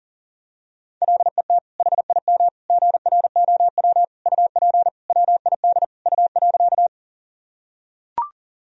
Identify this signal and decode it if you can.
Morse code, heard as a keyed tone, about 40 wpm, 700 Hz. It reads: LET HIM GROW UP WID U.
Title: Sentences from Top 300 Words and Abbreviations 40wpm